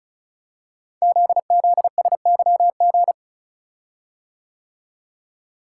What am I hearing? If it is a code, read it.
Morse code, heard as a keyed tone, about 35 wpm, 700 Hz. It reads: ZZSYG
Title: Code Groups - 5 Letters - 35wpm